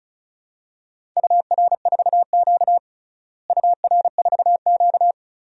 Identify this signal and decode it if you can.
Morse code, heard as a keyed tone, about 36 wpm, 700 Hz. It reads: UR4Q UR4Q
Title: DXCC-100 Call Signs - 4 Characters - Repeated 2x - 35wpm